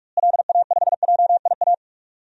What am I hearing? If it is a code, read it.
Morse code, heard as a keyed tone, about 48 wpm, 700 Hz. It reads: LA5JIA
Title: DXCC-100 Call Signs - 6 Characters - Repeated 2x - 45wpm